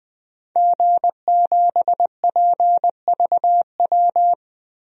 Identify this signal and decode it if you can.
Morse code, heard as a keyed tone, about 20 wpm, 700 Hz. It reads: G7PVW
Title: DXCC-100 Call Signs - 5 Characters - Repeated 3x - 20wpm